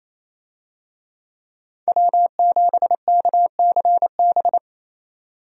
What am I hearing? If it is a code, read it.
Morse code, heard as a keyed tone, about 28 wpm, 700 Hz. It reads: W7KCB
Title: USA Call Signs - 5 Characters - 28wpm